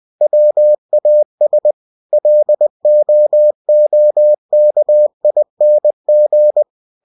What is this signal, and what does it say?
Morse code, heard as a keyed tone, about 20 wpm, 600 Hz. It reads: WAS LOOKING
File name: cw-books / winnie-the-pooh / winnie-20wpm-page-0087